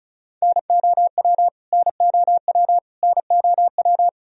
Morse code, heard as a keyed tone, about 34 wpm, 700 Hz. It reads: NOW NOW NOW
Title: Top 100 Words Repeated 35wpm